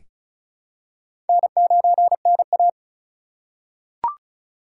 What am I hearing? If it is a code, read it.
Morse code, heard as a keyed tone, about 34 wpm, 700 Hz. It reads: N9NA
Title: USA Call Signs - 4 Characters - 6x Character Spacing - 35wpm